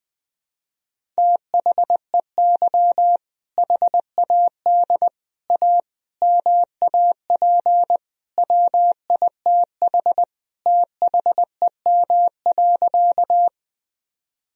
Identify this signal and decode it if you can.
Morse code, heard as a keyed tone, about 21 wpm, 700 Hz. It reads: THEY HAD A MAP WITH THEM.
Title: Sentences from Top 1000 Words - Review Part 2 - 20wpm